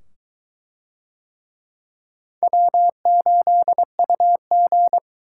Morse code, heard as a keyed tone, about 22 wpm, 700 Hz. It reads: W8UG